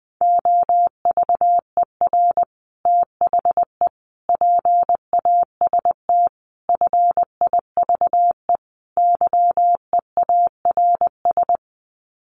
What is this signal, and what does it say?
Morse code, heard as a keyed tone, about 19 wpm, 700 Hz. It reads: OVER THE PAST FIVE YEARS
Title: Sets of 5 Words Encore 20wpm